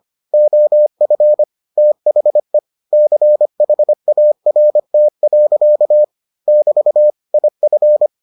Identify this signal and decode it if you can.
Morse code, heard as a keyed tone, about 25 wpm, 600 Hz. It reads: OF THE CHART. <BT> IF